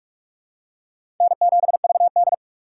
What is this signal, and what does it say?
Morse code, heard as a keyed tone, about 47 wpm, 700 Hz. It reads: N7VD
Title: USA Call Signs - 4 Characters - 2x Character Spacing - 45wpm